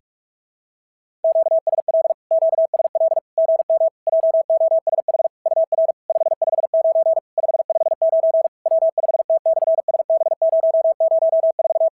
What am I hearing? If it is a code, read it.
Morse code, heard as a keyed tone, about 44 wpm, 650 Hz. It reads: QSL QSL GM JOSH UR 559 559 W5T/SB004
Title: SOTA QSO Practice 45wpm